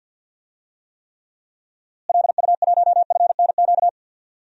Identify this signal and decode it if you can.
Morse code, heard as a keyed tone, about 52 wpm, 700 Hz. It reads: LU1FNQ
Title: DXCC-100 Call Signs - 6 Characters - Repeated 2x - 50wpm